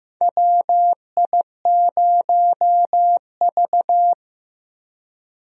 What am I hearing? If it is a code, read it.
Morse code, heard as a keyed tone, about 15 wpm, 700 Hz. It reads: WI0V
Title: USA Call Signs - 4 Characters - Repeated 2x - 15wpm